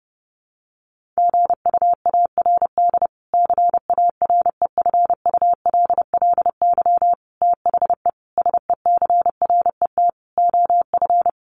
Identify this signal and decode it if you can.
Morse code, heard as a keyed tone, about 30 wpm, 700 Hz. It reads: GUARD CAREFULLY THE SECRET OF
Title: Sentences from Top 1300 Words 30wpm